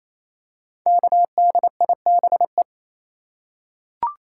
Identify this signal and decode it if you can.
Morse code, heard as a keyed tone, about 28 wpm, 700 Hz. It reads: KDIBE
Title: Code Groups - 5 Letters - 28wpm